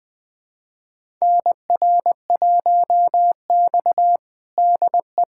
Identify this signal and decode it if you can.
Morse code, heard as a keyed tone, about 20 wpm, 700 Hz. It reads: NR1X DE
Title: USA Callsign DE Callsign - 4 Characters - 20wpm